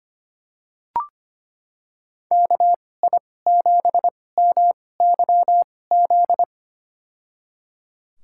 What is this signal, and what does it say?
Morse code, heard as a keyed tone, about 25 wpm, spaced as for 18 wpm, 700 Hz. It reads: KI7MYZ
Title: USA Call Signs - 6 Characters - 2x Character Spacing - 25wpm